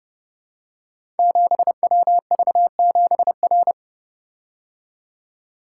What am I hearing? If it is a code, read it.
Morse code, heard as a keyed tone, about 28 wpm, 700 Hz. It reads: 7WV7R
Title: Code Groups - 5 Letter:Numbers - 30wpm